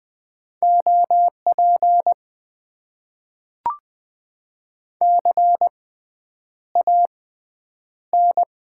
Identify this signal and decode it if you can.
Morse code, heard as a keyed tone, about 20 wpm, spaced as for 9 wpm, 700 Hz. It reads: OP CAN
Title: Top 90 QSO Elements - 6x Character Spacing - 20wpm